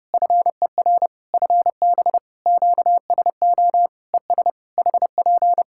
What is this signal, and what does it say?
Morse code, heard as a keyed tone, about 32 wpm, 700 Hz. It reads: FER FB QSO ES HP
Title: Practice QSO - Exchange 4 - Ending - 30wpm